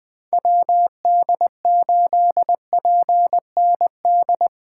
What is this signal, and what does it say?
Morse code, heard as a keyed tone, about 20 wpm, 700 Hz. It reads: WD8PND
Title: USA Call Signs - 6 Characters - 6x Character Spacing - 20wpm